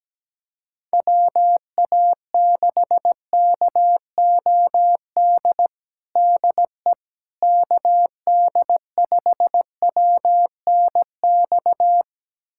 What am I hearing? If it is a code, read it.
Morse code, heard as a keyed tone, about 17 wpm, 700 Hz. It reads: WA6KOD DE KD5WNX
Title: USA Callsign DE Callsign - 6 Characters - 17wpm